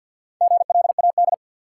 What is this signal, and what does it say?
Morse code, heard as a keyed tone, about 53 wpm, 700 Hz. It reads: GLAD
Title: Sentences from Top 800 Words - New Word Drill - 50wpm